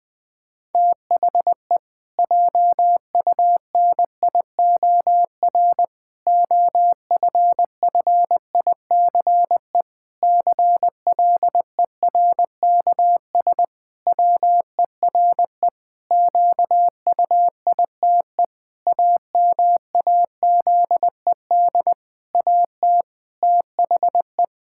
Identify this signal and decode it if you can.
Morse code, heard as a keyed tone, about 20 wpm, 700 Hz. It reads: THE JUNIOR OFFICE CLERKS WERE QUITE AMAZED AT THE
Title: Pangram Challenge v2 - 20wpm